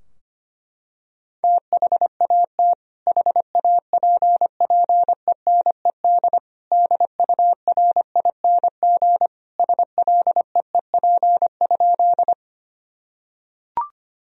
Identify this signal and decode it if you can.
Morse code, heard as a keyed tone, about 25 wpm, 700 Hz. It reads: THAT HAPPENED DURING SLEEP?
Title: Sentences from Top 1000 Words - Review Part 2 - 25wpm